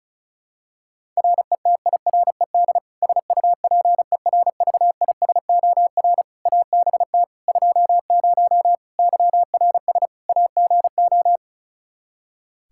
Morse code, heard as a keyed tone, about 34 wpm, 700 Hz. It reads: RETIRED SUPERVISOR ABT 20 YRS AGO
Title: Practice QSO - Exchange 3 - More Info - 35wpm